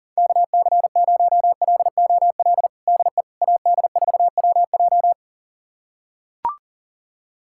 Morse code, heard as a keyed tone, about 40 wpm, 700 Hz. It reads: KC0LOL DE AD4WJ
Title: USA Callsign DE Callsign 40wpm